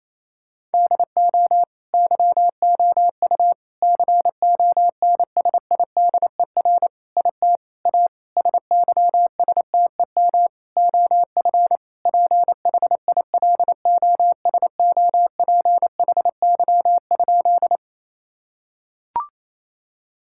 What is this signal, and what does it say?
Morse code, heard as a keyed tone, about 28 wpm, 700 Hz. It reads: DO YOU CONSIDER IT A SYSTEM OF PHILOSOPHY?
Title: Sentences from Top 1000 Words - Review Part 6 - 28wpm